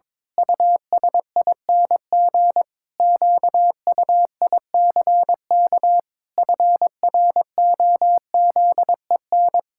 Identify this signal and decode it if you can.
Morse code, heard as a keyed tone, about 22 wpm, 700 Hz. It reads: USING QUICK FROZEN
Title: Pangram Challenge v2 - 22wpm